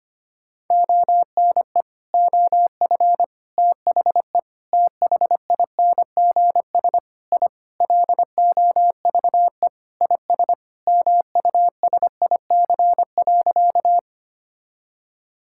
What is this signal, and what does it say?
Morse code, heard as a keyed tone, about 25 wpm, 700 Hz. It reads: ONE OF THE THINGS I LOVE IS MUSIC.
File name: practice / 25wpm / Sentences with 4-Word Phrases - 25wpm